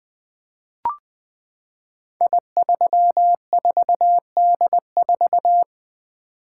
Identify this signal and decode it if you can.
Morse code, heard as a keyed tone, about 20 wpm, 700 Hz. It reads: I34D4
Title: Code Groups - 5 Letter:Numbers - 20wpm